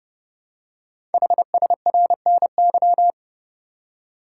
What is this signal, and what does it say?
Morse code, heard as a keyed tone, about 28 wpm, 700 Hz. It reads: HSRNY
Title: Code Groups - 5 Letters - 30wpm